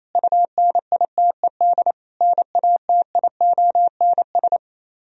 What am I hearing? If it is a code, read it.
Morse code, heard as a keyed tone, about 32 wpm, 700 Hz. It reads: UNITED NATIONS
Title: Speed-Racing - Sets of 3 Words 28wpm